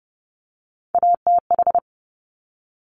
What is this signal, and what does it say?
Morse code, heard as a keyed tone, about 30 wpm, 700 Hz. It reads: ATH
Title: Top Three Letter Combinations in Words 30wpm